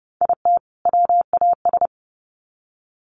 Morse code, heard as a keyed tone, about 30 wpm, 700 Hz. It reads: IT WAS
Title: Sets of 2 Words 30wpm